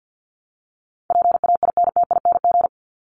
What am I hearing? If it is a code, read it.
Morse code, heard as a keyed tone, about 50 wpm, 700 Hz. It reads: PAINTING
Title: ING Suffix - Top 100 Words with Base Words 50wpm